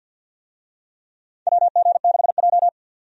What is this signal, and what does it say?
Morse code, heard as a keyed tone, about 53 wpm, 700 Hz. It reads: WG6J
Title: USA Call Signs - 4 Characters - 50wpm